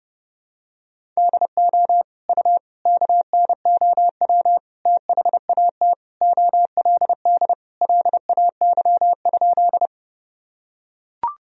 Morse code, heard as a keyed tone, about 30 wpm, 700 Hz. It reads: DO U KNOW THAT OLD LAY?
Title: Sentences from Top 400 Words and Abbreviations 30wpm